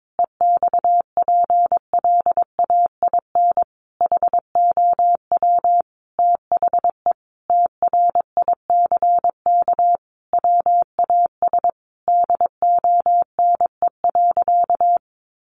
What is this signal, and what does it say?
Morse code, heard as a keyed tone, about 22 wpm, 700 Hz. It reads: EXPLAIN HOW THE TRICK WAS DONE.
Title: Sentences from Top 1500 Words 22wpm